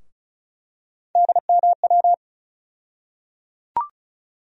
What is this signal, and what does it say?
Morse code, heard as a keyed tone, about 36 wpm, 700 Hz. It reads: DMW
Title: USA Call Sign Postfixes - 3 Letters 35wpm